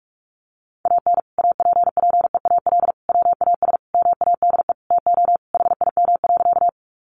Morse code, heard as a keyed tone, about 44 wpm, 700 Hz. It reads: AN APPEAL WAS MADE TO HIM.
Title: Sentences from Top 1300 Words 45wpm